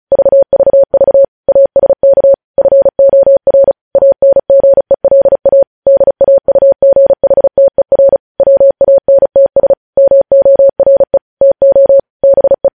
Morse code, heard as a keyed tone, about 36 wpm, 550 Hz. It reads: VVV ASK FOR ANGELA DAUGHTER WANTS MORE TO BE
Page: News headlines in Morse code at 35 wpm